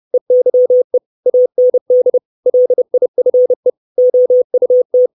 Morse code, heard as a keyed tone, about 30 wpm, 500 Hz. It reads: EYE AND LIFE OUT
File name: 632 Words 30WPM